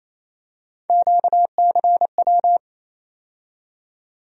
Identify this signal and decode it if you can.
Morse code, heard as a keyed tone, about 27 wpm, 700 Hz. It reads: QCW
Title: USA Call Sign Postfixes - 3 Letters 28wpm